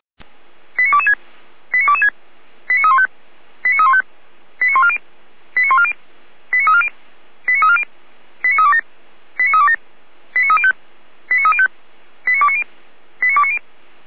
Five-tone selective call (ZVEI), heard as quick runs of five tones, tones of about 70 ms. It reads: 89226 89226 89315 89315 89140 89140 89340 89340 89327 89327 89305 89305 89290 89290